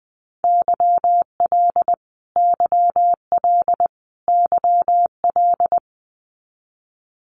Morse code, heard as a keyed tone, about 20 wpm, 700 Hz. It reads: YL YL YL
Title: Abbreviations Repeated 3x - 20wpm